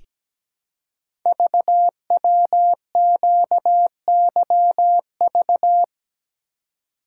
Morse code, heard as a keyed tone, about 17 wpm, 700 Hz. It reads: VWQYV
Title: Code Groups - 5 Letters - 17wpm